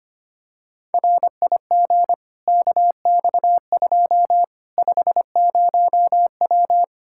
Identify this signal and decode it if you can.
Morse code, heard as a keyed tone, about 25 wpm, 700 Hz. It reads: RIG KX2 50W
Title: Practice QSO - Exchange 2 - Info - 25wpm